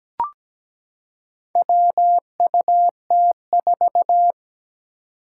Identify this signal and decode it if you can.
Morse code, heard as a keyed tone, about 17 wpm, 700 Hz. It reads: WUT4
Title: Four Challenging Letter-Number Combinations 17wpm